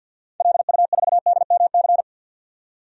Morse code, heard as a keyed tone, about 51 wpm, 700 Hz. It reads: LU4DMC
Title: DXCC-100 Call Signs - 6 Characters - 50wpm